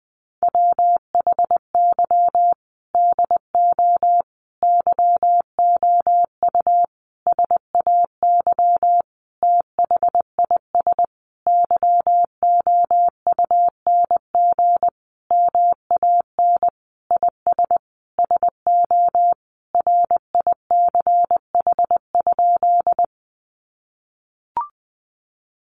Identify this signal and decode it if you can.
Morse code, heard as a keyed tone, about 19 wpm, 700 Hz. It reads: WHY DO YOU SAY THIS YOUNG MAN IS SO RICH?
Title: Sentences from Top 1100 Words 20wpm